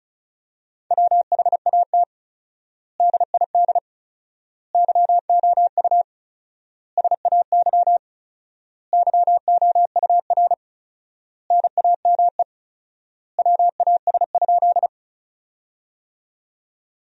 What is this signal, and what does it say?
Morse code, heard as a keyed tone, about 34 wpm, 700 Hz. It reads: WHAT DID YOU SAY YOUR NAME WAS?
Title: 4x Word-Spacing - Sentences from Top 200 Words - 35wpm